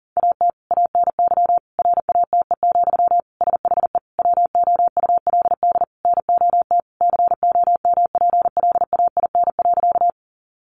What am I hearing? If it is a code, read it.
Morse code, heard as a keyed tone, about 40 wpm, 700 Hz. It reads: AT ANY RATE, SHE WOULD NOT COMPLAIN.
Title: Sentences from Top 1300 Words 40wpm